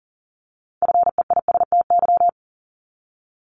Morse code, heard as a keyed tone, about 37 wpm, 700 Hz. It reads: FEISTY
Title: Multi-letter Phonemes Repeated 3x 40wpm